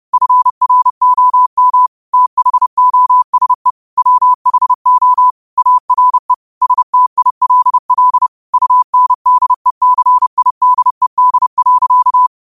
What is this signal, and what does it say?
Morse code, heard as a keyed tone, about 30 wpm, 1000 Hz. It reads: FROM THOSE WHO ARE STILL UNDECIDED.
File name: qotdfiles / 30wpm